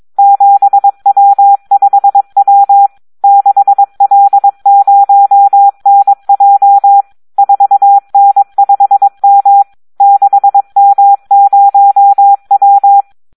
Morse code, heard as a keyed tone, about 22 wpm, 800 Hz. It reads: 7W5W 6L0NJ 4N5M 6M0W
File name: Audio practica CW / SD 1 22